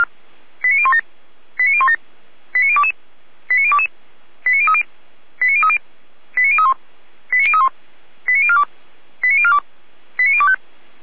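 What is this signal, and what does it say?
Five-tone selective call (ZVEI), heard as quick runs of five tones, tones of about 70 ms. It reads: 89017 89017 89022 89022 89030 89030 89031 89031 89042 89042 89025
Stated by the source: ZVEI five-tone selective call received on an SDR.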